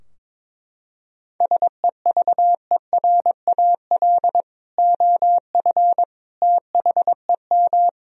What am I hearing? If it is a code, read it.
Morse code, heard as a keyed tone, about 22 wpm, 700 Hz. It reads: SEVERAL OF THEM
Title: Sentences from Top 1000 Words - Review Part 2 - 22wpm